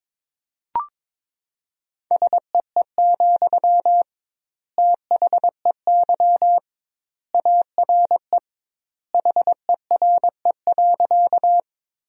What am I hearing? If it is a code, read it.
Morse code, heard as a keyed tone, about 22 wpm, 700 Hz. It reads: SEE, THEY ARE HERE.
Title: 2x Word-Spacing - Sentences from Top 200 Words - 22wpm